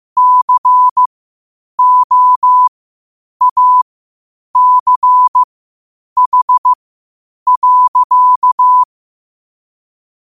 Morse code, heard as a keyed tone, about 15 wpm, spaced as for 8 wpm, 1000 Hz. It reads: COACH.
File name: qotdfiles / 8wpm